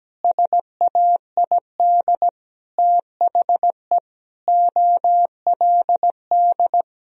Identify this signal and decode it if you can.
Morse code, heard as a keyed tone, about 17 wpm, 700 Hz. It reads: SAID THE OLD
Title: Sentences from Top 1000 Words - Review Part 2 - 17wpm